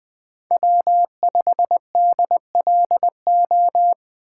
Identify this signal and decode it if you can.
Morse code, heard as a keyed tone, about 20 wpm, 700 Hz. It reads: W5DLO
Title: USA Call Signs - 5 Characters - 4x Character Spacing - 20wpm